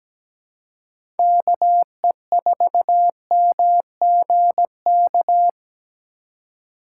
Morse code, heard as a keyed tone, about 17 wpm, 700 Hz. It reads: KE4MGK
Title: USA Call Signs - 6 Characters - 4x Character Spacing - 17wpm